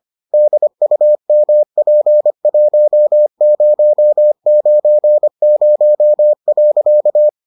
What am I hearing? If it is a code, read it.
Morse code, heard as a keyed tone, about 25 wpm, 600 Hz. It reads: DUMP1090.